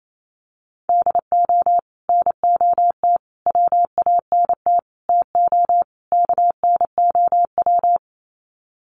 Morse code, heard as a keyed tone, about 28 wpm, 700 Hz. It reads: DO NOT WANT TO KNOW
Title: Sets of 5 Words Encore 28wpm